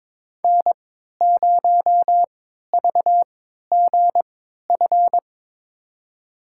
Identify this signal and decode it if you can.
Morse code, heard as a keyed tone, about 22 wpm, spaced as for 13 wpm, 700 Hz. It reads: N0VGF
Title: USA Call Signs - 5 Characters - 3x Character Spacing - 22wpm